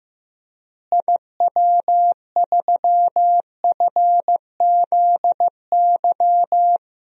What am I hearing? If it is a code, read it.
Morse code, heard as a keyed tone, about 15 wpm, 700 Hz. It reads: IW3FZY
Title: DXCC-100 Call Signs - 6 Characters - Repeated 3x - 15wpm